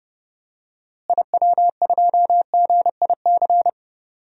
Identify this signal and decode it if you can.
Morse code, heard as a keyed tone, about 32 wpm, 700 Hz. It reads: IW2GIC
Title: DXCC-100 Call Signs - 6 Characters - Repeated 2x - 30wpm